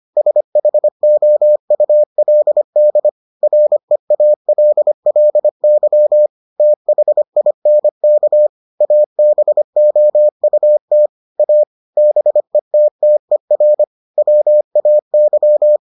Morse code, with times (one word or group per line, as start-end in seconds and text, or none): 0.2-3.1 SHOULD
3.4-6.3 REALLY
6.6-8.5 THINK
8.8-11.1 ABOUT
11.4-11.6 A
12.0-13.8 BETTER
14.2-15.8 WAY